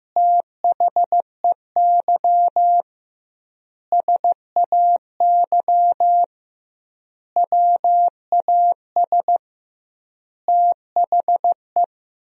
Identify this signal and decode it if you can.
Morse code, heard as a keyed tone, about 15 wpm, 700 Hz. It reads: THEY SAY WAS THE
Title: 2x Word-Spacing - Sentences from Top 200 Words - 15wpm